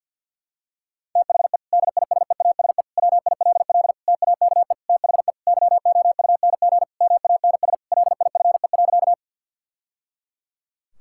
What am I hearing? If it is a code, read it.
Morse code, heard as a keyed tone, about 46 wpm, 700 Hz. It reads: THE DISEASE WILL TAKE THE YOUNG MANS LIFE.
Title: Sentences from Top 1000 Words - Review Part 6 - 50wpm